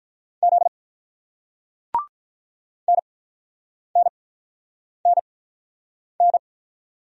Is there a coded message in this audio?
Yes: Morse code from 0.4 to 6.4 s, heard as a keyed tone, about 48 wpm, spaced as for 9 wpm, 700 Hz. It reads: Z NNNN